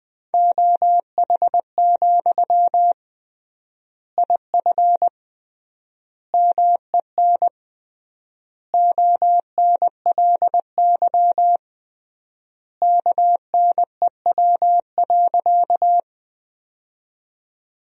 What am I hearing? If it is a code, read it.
Morse code, heard as a keyed tone, about 19 wpm, 700 Hz. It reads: OH, IF MEN ONLY KNEW.
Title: 3x Word-Spacing - Sentences from Top 500 Words - 20wpm